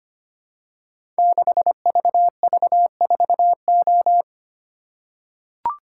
Morse code, heard as a keyed tone, about 25 wpm, 700 Hz. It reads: 6VV4O
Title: Code Groups - 5 Letter:Numbers - 25wpm